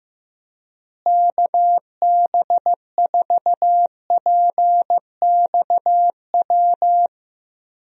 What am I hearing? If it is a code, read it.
Morse code, heard as a keyed tone, about 15 wpm, 700 Hz. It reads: KB4PXW
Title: USA Call Signs - 6 Characters - Repeated 2x - 15wpm